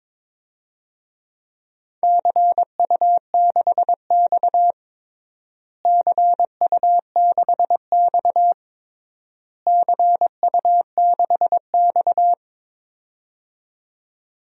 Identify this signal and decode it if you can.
Morse code, heard as a keyed tone, about 22 wpm, 700 Hz. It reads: CU6X CU6X CU6X
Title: DXCC-100 Call Signs - 4 Characters - Repeated 3x - 22wpm